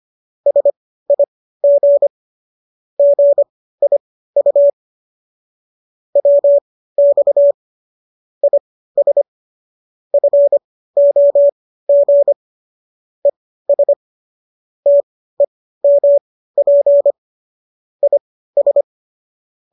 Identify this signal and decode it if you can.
Morse code, heard as a keyed tone, about 25 wpm, spaced as for 15 wpm, 570 Hz. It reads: SIG GIU WX IS FOG ES TEMP IS